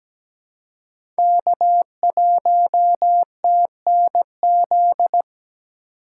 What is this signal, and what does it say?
Morse code, heard as a keyed tone, about 17 wpm, 700 Hz. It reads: K1TNZ
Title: USA Call Signs - 5 Characters - 4x Character Spacing - 17wpm